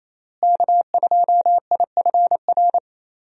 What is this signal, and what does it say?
Morse code, heard as a keyed tone, about 28 wpm, 700 Hz. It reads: K2IFR
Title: USA Call Signs - 5 Characters - Repeated 3x - 28wpm